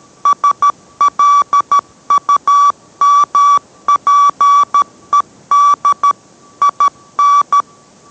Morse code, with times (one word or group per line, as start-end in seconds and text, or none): 0.3-6.1 SLUMPED
6.6-7.6 IN